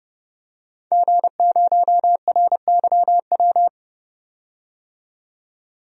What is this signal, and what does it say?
Morse code, heard as a keyed tone, about 30 wpm, 700 Hz. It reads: G0RYW
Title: DXCC-100 Call Signs - 5 Characters - 30wpm